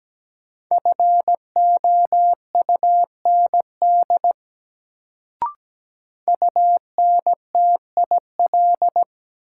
Morse code, heard as a keyed tone, about 17 wpm, 700 Hz. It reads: FOUND UNTIL
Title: Sets of 5 Words Encore - New Word Drill - 17wpm